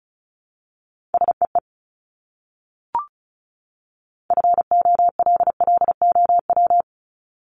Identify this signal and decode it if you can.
Morse code, heard as a keyed tone, about 34 wpm, 700 Hz. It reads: SEE FOLLOW
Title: Top 500 Words 35wpm